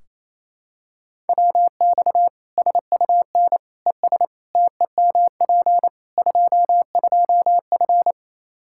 Morse code, heard as a keyed tone, about 28 wpm, 700 Hz. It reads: WX SUN ES TEMP 22F
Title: Practice QSO - Exchange 2 - Info - 28wpm